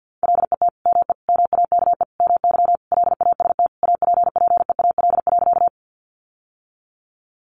Morse code, heard as a keyed tone, about 50 wpm, 700 Hz. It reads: LET ME MAKE MY LAST APPEAL.
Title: Sentences from Top 1300 Words 50wpm